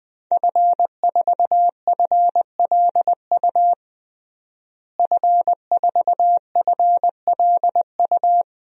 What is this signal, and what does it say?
Morse code, heard as a keyed tone, about 20 wpm, 700 Hz. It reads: F4FLU F4FLU
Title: DXCC-100 Call Signs - 5 Characters - Repeated 3x - 20wpm